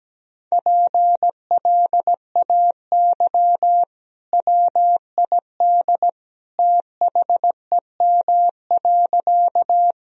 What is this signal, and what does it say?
Morse code, heard as a keyed tone, about 17 wpm, 700 Hz. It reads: PLAY WID THEM.
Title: Sentences from Top 200 Words and Abbreviations 17wpm